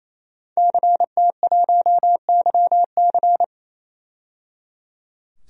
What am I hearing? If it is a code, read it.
Morse code, heard as a keyed tone, about 28 wpm, 700 Hz. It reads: CT1YC
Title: DXCC-100 Call Signs - 5 Characters - 28wpm